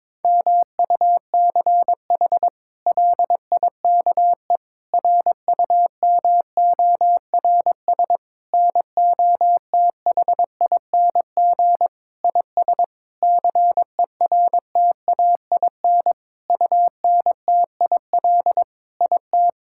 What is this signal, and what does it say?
Morse code, heard as a keyed tone, about 22 wpm, 700 Hz. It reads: MUCH LIKE RUMORS NOTHING IS CERTAIN UNTIL IT
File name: practice / 22wpm / Sentences with 8-Word Phrases - 22wpm